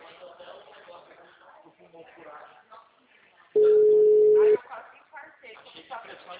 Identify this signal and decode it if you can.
Ringback tone (Europe, 425 Hz), with speech.